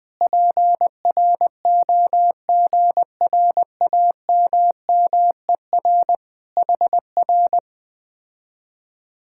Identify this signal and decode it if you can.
Morse code, heard as a keyed tone, about 20 wpm, 700 Hz. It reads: PROGRAMMER HR
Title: Speed-Racing Common QSO Phrases Encore - 17wpm